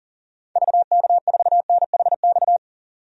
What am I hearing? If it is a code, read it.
Morse code, heard as a keyed tone, about 37 wpm, 700 Hz. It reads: VK4NHX
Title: DXCC-100 Call Signs - 6 Characters - 40wpm